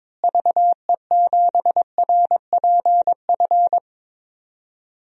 Morse code, heard as a keyed tone, about 22 wpm, 700 Hz. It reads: VE7RPF
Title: DXCC-100 Call Signs - 6 Characters - Repeated 2x - 22wpm